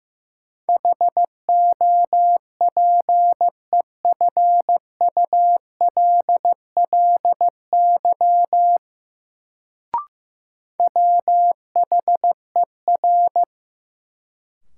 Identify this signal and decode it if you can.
Morse code, heard as a keyed tone, about 15 wpm, 700 Hz. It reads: HOPEFULLY WHER
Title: Sentences from Top 1500 Words - New Word Drill Alt - 15wpm